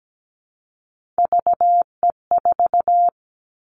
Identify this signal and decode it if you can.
Morse code, heard as a keyed tone, about 17 wpm, 700 Hz. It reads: VE4
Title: Callsign Prefixes for Top 25 DXCC - Repeated 3x - 17wpm